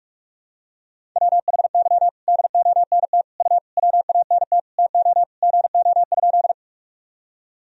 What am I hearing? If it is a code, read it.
Morse code, heard as a keyed tone, about 44 wpm, 700 Hz. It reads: WHY DONT U WANT TO GO?
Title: Sentences from Top 300 Words and Abbreviations 45wpm